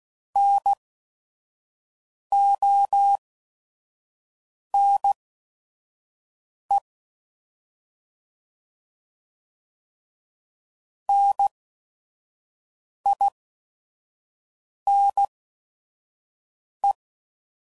Morse code, heard as a keyed tone, about 16 wpm, spaced as for 5 wpm, 800 Hz. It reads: NONE NINE